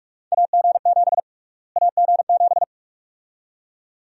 Morse code, heard as a keyed tone, about 44 wpm, 700 Hz. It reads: AG7 AG7
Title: USA Call Sign Prefixes - Repeated 2x - 45wpm